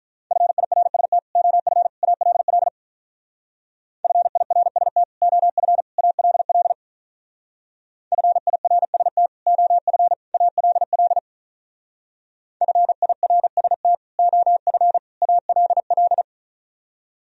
Morse code, heard as a keyed tone, about 44 wpm, 700 Hz. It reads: FIRST OF ALL FIRST OF ALL FIRST OF ALL FIRST OF ALL
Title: Speed-Racing - Sets of 3 Words 35wpm